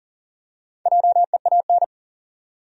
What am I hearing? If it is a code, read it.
Morse code, heard as a keyed tone, about 40 wpm, 700 Hz. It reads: JEAN
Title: USA Common Names - Max 5 Letters - 40wpm